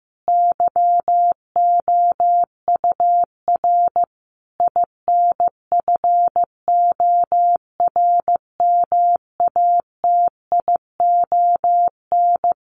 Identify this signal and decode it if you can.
Morse code, heard as a keyed tone, about 15 wpm, 700 Hz. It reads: YOUR INFORMATION
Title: Sentences from Top 600 Words 15wpm